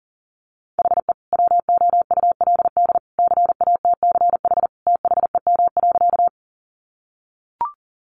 Morse code, heard as a keyed tone, about 40 wpm, 700 Hz. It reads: HE WOULD CATCH THEM.
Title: Sentences from Top 1500 Words 40wpm